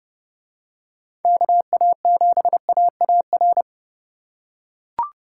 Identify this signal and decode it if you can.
Morse code, heard as a keyed tone, about 29 wpm, 700 Hz. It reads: KA7AAR